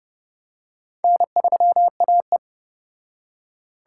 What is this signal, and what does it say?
Morse code, heard as a keyed tone, about 29 wpm, 700 Hz. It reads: N3AE